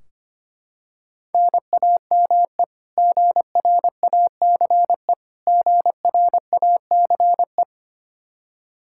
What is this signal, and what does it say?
Morse code, heard as a keyed tone, about 25 wpm, 700 Hz. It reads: NAME GRACE GRACE
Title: Practice QSO - Exchange 1 - Beginning - 25wpm